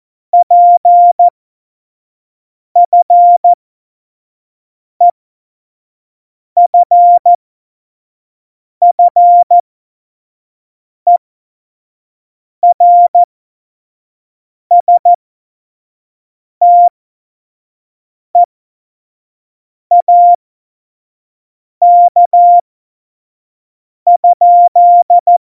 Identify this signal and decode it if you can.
Morse code, heard as a keyed tone, about 14 wpm, spaced as for 5 wpm, 700 Hz. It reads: PFEFFERSTEAK?
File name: Morsekurs / Witze / Witz 27. August 2019, 25 F70